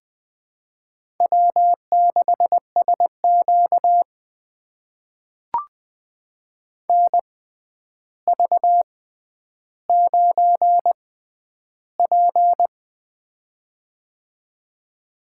Morse code, heard as a keyed tone, about 20 wpm, spaced as for 10 wpm, 700 Hz. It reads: W6SQ NV9P
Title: USA Call Signs - 4 Characters - 6x Character Spacing - 20wpm